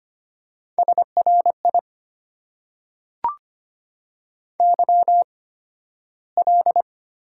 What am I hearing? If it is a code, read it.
Morse code, heard as a keyed tone, about 25 wpm, spaced as for 13 wpm, 700 Hz. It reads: SRI YL
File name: Top 90 QSO Elements - 8x Character Spacing - 25wpm